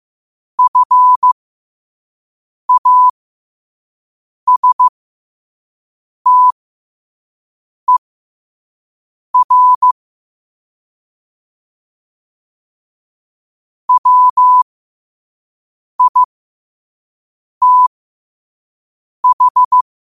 Morse code, heard as a keyed tone, about 15 wpm, spaced as for 5 wpm, 1000 Hz. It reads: FASTER WITH
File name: qotdfiles / 5wpm